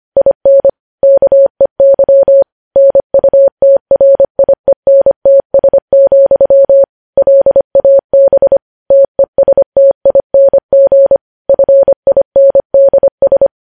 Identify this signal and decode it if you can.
Morse code, heard as a keyed tone, about 25 wpm, 550 Hz. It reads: IN KEY NUTRIENTS, LAB TESTING FINDS
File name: podcast / News headlines-1x-25x25